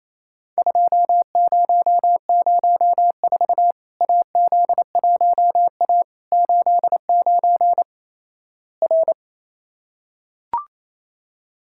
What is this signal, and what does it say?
Morse code, heard as a keyed tone, about 28 wpm, 700 Hz. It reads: 2004 AZ1A 89 R
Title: Worked All Europe DX Contest - QTC Exchange - 28wpm